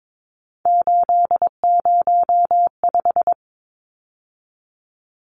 Morse code, heard as a keyed tone, about 22 wpm, 700 Hz. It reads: 805
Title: Numbers - 3-Digit 22wpm